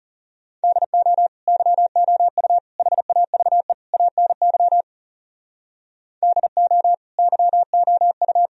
Morse code, heard as a keyed tone, about 38 wpm, 700 Hz. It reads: DO YOU HAVE ANY DO YOU